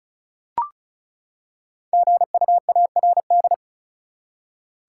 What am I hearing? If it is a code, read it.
Morse code, heard as a keyed tone, about 34 wpm, 700 Hz. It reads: GUARD
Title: Five Letter Words - 35wpm